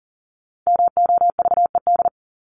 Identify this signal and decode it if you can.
Morse code, heard as a keyed tone, about 40 wpm, 700 Hz. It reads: MOVED
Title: ED Suffix - Top 100 Words with Base Words 40wpm